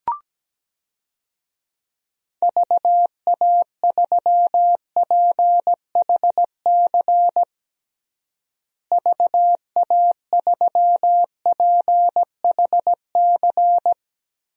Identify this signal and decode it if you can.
Morse code, heard as a keyed tone, about 17 wpm, 700 Hz. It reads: VA3PHC VA3PHC
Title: DXCC-100 Call Signs - 6 Characters - Repeated 3x - 17wpm